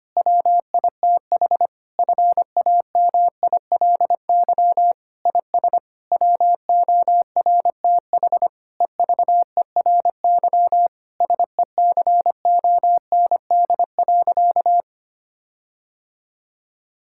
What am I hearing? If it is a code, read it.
Morse code, heard as a keyed tone, about 25 wpm, 700 Hz. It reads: WITH FAMILY IS WORTH EVERY SECOND.
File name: practice / 25wpm / Difficult Sentences with Hints - 25wpm